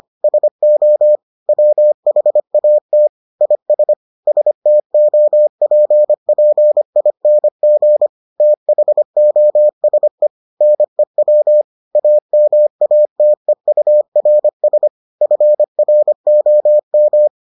Morse code, heard as a keyed tone, about 25 wpm, 600 Hz. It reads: SO WHAT IS STOPPING THOSE NEW AMATEURS FROM